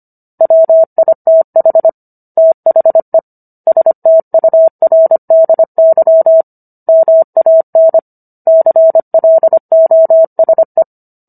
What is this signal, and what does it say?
Morse code, heard as a keyed tone, about 25 wpm, 650 Hz. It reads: WITH THE STURDY MAN CLOSE